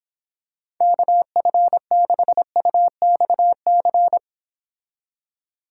Morse code, heard as a keyed tone, about 25 wpm, 700 Hz. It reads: KF6UXC